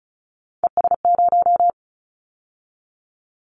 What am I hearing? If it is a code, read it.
Morse code, heard as a keyed tone, about 34 wpm, 700 Hz. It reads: ES0
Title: Callsign Prefixes for Top 100 DXCC - 35wpm